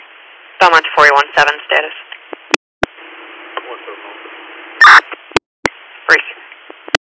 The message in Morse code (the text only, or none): none